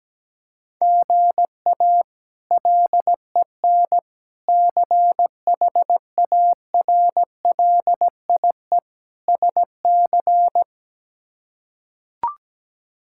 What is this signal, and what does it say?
Morse code, heard as a keyed tone, about 17 wpm, 700 Hz. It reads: GA LEN CHARLIE SC
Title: K1USN SST QSO Practice - 17wpm